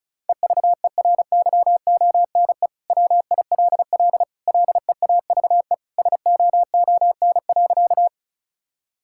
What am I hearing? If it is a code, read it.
Morse code, heard as a keyed tone, about 36 wpm, 700 Hz. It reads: EVERYONE WILL LEAVE SOON.
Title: Sentences from Top 1000 Words - Review Part 5 - 35wpm